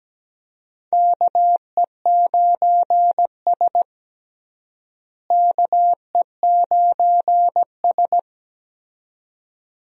Morse code, heard as a keyed tone, about 17 wpm, 700 Hz. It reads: KE9S KE9S
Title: USA Call Signs - 4 Characters - Repeated 2x - 17wpm